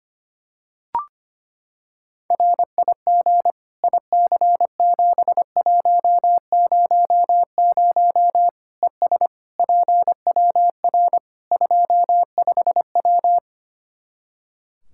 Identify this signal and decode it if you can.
Morse code, heard as a keyed tone, about 25 wpm, 700 Hz. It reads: RIG IC7100 ES PWR 25W
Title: K1USN SST QSO Practice 25wpm